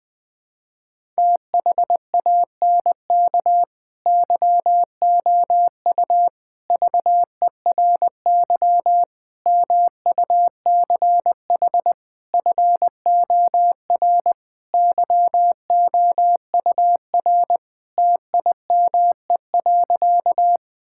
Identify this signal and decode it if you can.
Morse code, heard as a keyed tone, about 20 wpm, 700 Hz. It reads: THANK YOU VERY MUCH FOR YOUR TIME.
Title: Sentences with 5-Word Phrases - Spoken Beginning - 20wpm